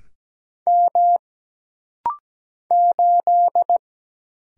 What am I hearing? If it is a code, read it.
Morse code, heard as a keyed tone, about 17 wpm, 700 Hz. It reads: M 8
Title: Single Letter-Number - Rapid-Fire 17wpm